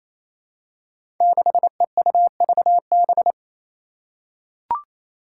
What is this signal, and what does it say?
Morse code, heard as a keyed tone, about 28 wpm, 700 Hz. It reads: 6EUVB
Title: Five Challenging Letter-Number Combinations 28wpm